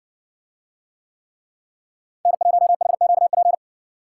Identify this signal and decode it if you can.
Morse code, heard as a keyed tone, about 56 wpm, 700 Hz. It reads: N1HQP